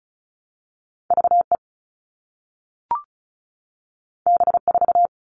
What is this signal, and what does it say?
Morse code, heard as a keyed tone, about 36 wpm, 700 Hz. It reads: VE B4
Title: All Cut Numbers - 2-Digit 35wpm